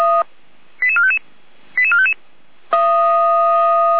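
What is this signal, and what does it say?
Five-tone selective call (ZVEI), heard as quick runs of five tones, tones of about 70 ms. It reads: 88455 88455
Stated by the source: ZVEI five-tone selective call received on SDR.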